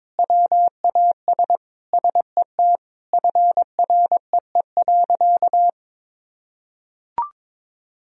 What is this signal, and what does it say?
Morse code, heard as a keyed tone, about 22 wpm, 700 Hz. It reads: WAS SET FREE.
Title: Sentences from Top 1000 Words - Review Part 1 - 22wpm